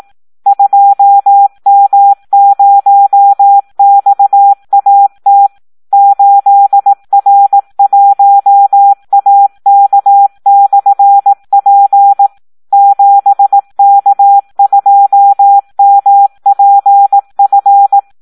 Morse code, heard as a keyed tone, about 18 wpm, 800 Hz. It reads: 2M0XAT 8R1AK/P 7K2MPF
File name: Audio practica CW / SD 1 18